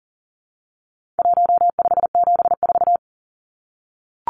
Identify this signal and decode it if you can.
Morse code, heard as a keyed tone, about 40 wpm, 700 Hz. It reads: 1574
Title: Numbers - 4-Digit 40wpm